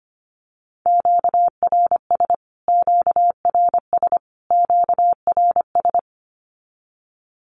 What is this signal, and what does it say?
Morse code, heard as a keyed tone, about 25 wpm, 700 Hz. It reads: QRS QRS QRS
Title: Q-Codes Repeated 3x 25wpm